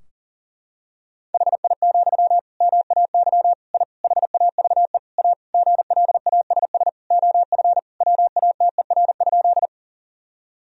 Morse code, heard as a keyed tone, about 40 wpm, 700 Hz. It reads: HI, MAY I HAVE A GLASS OF WATER?
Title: Sentences from Top 1000 Words - Review Part 3 - 40wpm